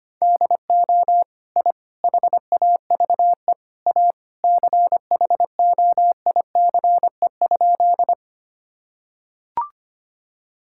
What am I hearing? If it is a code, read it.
Morse code, heard as a keyed tone, about 25 wpm, 700 Hz. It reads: DO I HAVE A CHOICE?
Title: Sentences from Top 1000 Words - Review Part 6 - 25wpm